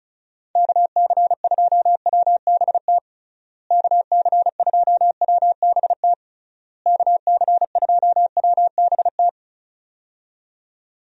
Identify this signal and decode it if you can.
Morse code, heard as a keyed tone, about 34 wpm, 700 Hz. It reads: KC2WBT KC2WBT KC2WBT
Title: USA Call Signs - 6 Characters - Repeated 3x - 35wpm